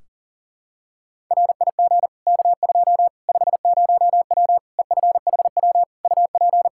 Morse code, heard as a keyed tone, about 40 wpm, 700 Hz. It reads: RIG K2 50W EFHW UP
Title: Practice QSO - Exchange 2 - Info - 40wpm